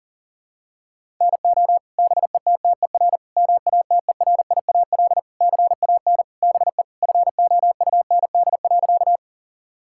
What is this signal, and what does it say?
Morse code, heard as a keyed tone, about 40 wpm, 700 Hz. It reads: NO BETTER MATERIAL CAN BE FOUND.
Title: Sentences from Top 1000 Words - Review Part 5 - 40wpm